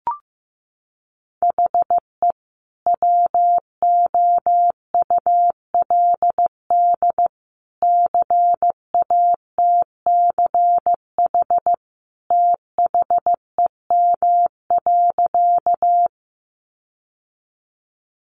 Morse code, heard as a keyed tone, about 14 wpm, 700 Hz. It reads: HE WOULD CATCH THEM.
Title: Sentences from Top 1500 Words 15wpm